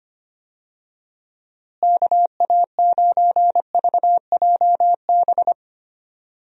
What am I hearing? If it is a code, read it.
Morse code, heard as a keyed tone, about 25 wpm, 700 Hz. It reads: KA9VJB
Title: USA Call Signs - 6 Characters - Repeated 2x - 25wpm